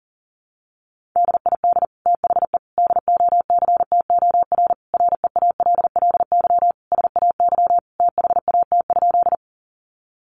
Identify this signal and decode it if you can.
Morse code, heard as a keyed tone, about 40 wpm, 700 Hz. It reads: DID THE DOCTOR REALLY SAY THAT?
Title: Sentences from Top 1500 Words 40wpm